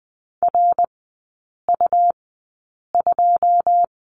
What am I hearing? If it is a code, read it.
Morse code, heard as a keyed tone, about 20 wpm, spaced as for 8 wpm, 690 Hz. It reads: RU2
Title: 205 Lesson 33e - Letter j - Three Characters 20wpm